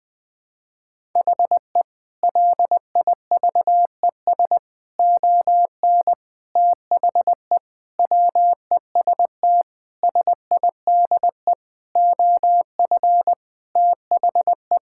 Morse code, heard as a keyed tone, about 21 wpm, 700 Hz. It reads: HE LIVES ON THE WEST SIDE OF THE
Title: Sentences with 6-Word Phrases - 20wpm